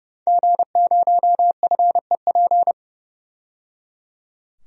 Morse code, heard as a keyed tone, about 30 wpm, 700 Hz. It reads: G0FEP
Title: DXCC-100 Call Signs - 5 Characters - 30wpm